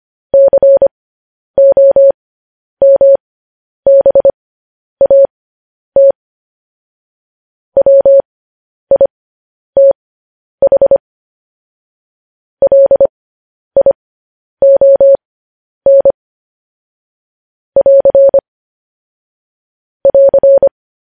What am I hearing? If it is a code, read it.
Morse code, heard as a keyed tone, about 25 wpm, spaced as for 10 wpm, 550 Hz. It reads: COMBAT WITH LION <AR> <AR>
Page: news headlines in morse code at 10 wpm